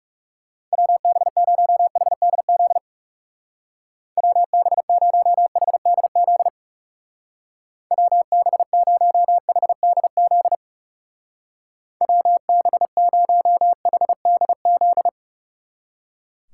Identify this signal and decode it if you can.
Morse code, heard as a keyed tone, about 36 wpm, 700 Hz. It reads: WB0HDZ WB0HDZ WB0HDZ WB0HDZ
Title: Speed Racing - USA 2x3 Call Signs 30wpm